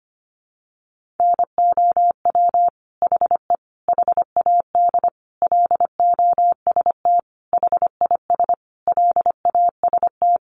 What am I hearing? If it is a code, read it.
Morse code, heard as a keyed tone, about 25 wpm, 700 Hz. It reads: NOW HE HAD LOST HIS LAST
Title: Sentences from Top 1400 Words 25wpm